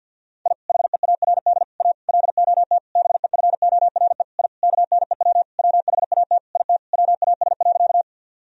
Morse code, heard as a keyed tone, about 50 wpm, 700 Hz. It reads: I HEARD A LOT BEFORE I KNEW WHAT IT WAS.
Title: Sentences from Top 1000 Words - Review Part 3 - 50wpm